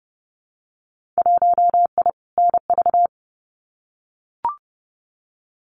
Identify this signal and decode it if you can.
Morse code, heard as a keyed tone, about 30 wpm, 700 Hz. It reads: 1I NV
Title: Winter Field Day Exchanges Encore 30wpm